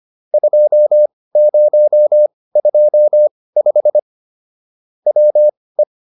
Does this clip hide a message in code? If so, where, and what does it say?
Yes: Morse code from 0.3 to 5.8 s, heard as a keyed tone, about 25 wpm, spaced as for 18 wpm, 600 Hz. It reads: 2025 WE